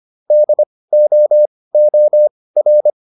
Morse code, heard as a keyed tone, about 25 wpm, spaced as for 18 wpm, 600 Hz. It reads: DOOR
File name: PR305 18